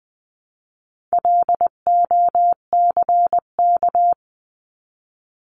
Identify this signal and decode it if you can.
Morse code, heard as a keyed tone, about 20 wpm, 700 Hz. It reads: LOCK
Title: 259 Lesson 42d - Words 18-20wpm